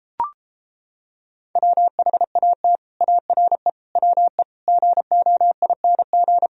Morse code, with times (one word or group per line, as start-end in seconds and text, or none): 1.6-2.8 WHAT
3.0-3.7 ARE
4.0-4.4 WE
4.7-6.5 GOING